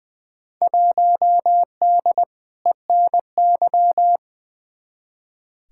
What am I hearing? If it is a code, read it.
Morse code, heard as a keyed tone, about 20 wpm, 700 Hz. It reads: 1D ENY
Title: Field Day Exchanges - 20wpm